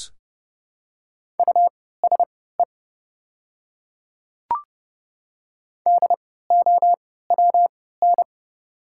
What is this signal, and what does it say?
Morse code, heard as a keyed tone, about 31 wpm, spaced as for 17 wpm, 690 Hz. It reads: USE DOWN
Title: 3x Character Spacing - Top 200 Words - 30wpm